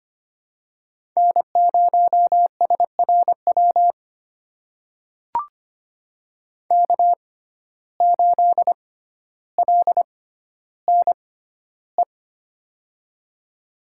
Morse code, heard as a keyed tone, about 25 wpm, spaced as for 13 wpm, 700 Hz. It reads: N0SRW K8LNE
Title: USA Call Signs - 5 Characters - 6x Character Spacing - 25wpm